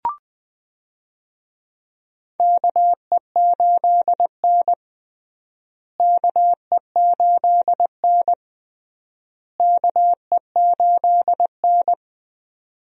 Morse code, heard as a keyed tone, about 21 wpm, 700 Hz. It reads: KE8N KE8N KE8N
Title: USA Call Signs - 4 Characters - Repeated 3x - 20wpm